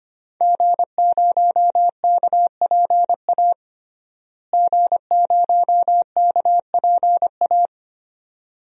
Morse code, heard as a keyed tone, about 25 wpm, 700 Hz. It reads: G0KPA G0KPA
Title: DXCC-100 Call Signs - 5 Characters - Repeated 2x - 25wpm